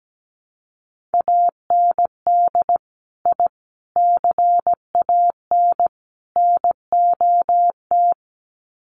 Morse code, heard as a keyed tone, about 17 wpm, 700 Hz. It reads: AND I CAN NOT
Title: Sets of 4 Words 17wpm